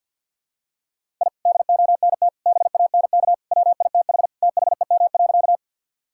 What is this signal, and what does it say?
Morse code, heard as a keyed tone, about 50 wpm, 700 Hz. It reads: I DONT BANK WITH THEM.
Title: Sentences from Top 1000 Words - Review Part 6 - 50wpm